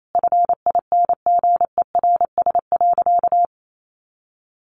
Morse code, heard as a keyed tone, about 28 wpm, 700 Hz. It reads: FINGERS.
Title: A Sound Of Thunder 28wpm